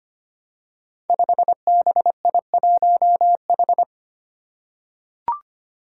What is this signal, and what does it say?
Morse code, heard as a keyed tone, about 25 wpm, 700 Hz. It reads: 5BI1H
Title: Five Challenging Letter-Number Combinations 25wpm